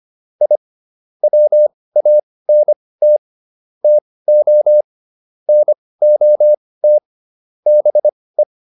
Morse code, heard as a keyed tone, about 25 wpm, spaced as for 18 wpm, 600 Hz. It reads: I WANT TO NOT BE
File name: SS108 18